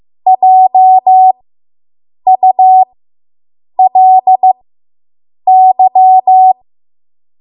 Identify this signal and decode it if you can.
Morse code, heard as a keyed tone, about 15 wpm, spaced as for 7 wpm, 750 Hz. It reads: JULY